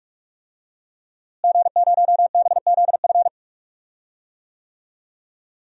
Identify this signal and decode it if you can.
Morse code, heard as a keyed tone, about 46 wpm, 700 Hz. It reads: G0BZF
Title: DXCC-100 Call Signs - 5 Characters - Repeated 2x - 45wpm